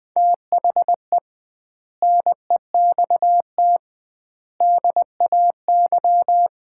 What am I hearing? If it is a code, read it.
Morse code, heard as a keyed tone, about 20 wpm, 700 Hz. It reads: THE NEXT DAY
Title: 2x Word-Spacing - Sets of 3 Words - 20wpm